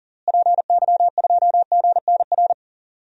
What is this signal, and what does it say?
Morse code, heard as a keyed tone, about 40 wpm, 700 Hz. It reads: PY2GNR
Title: DXCC-100 Call Signs - 6 Characters - Repeated 2x - 40wpm